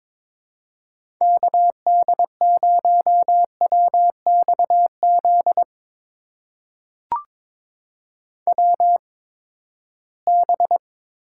Morse code, heard as a keyed tone, about 22 wpm, spaced as for 15 wpm, 700 Hz. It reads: KD0WXZ WB
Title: USA Call Signs - 6 Characters - 8x Character Spacing - 22wpm